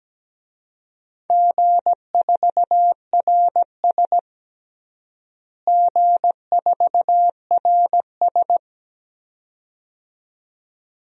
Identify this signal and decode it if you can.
Morse code, heard as a keyed tone, about 17 wpm, 700 Hz. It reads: G4RS G4RS
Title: DXCC-100 Call Signs - 4 Characters - Repeated 2x - 17wpm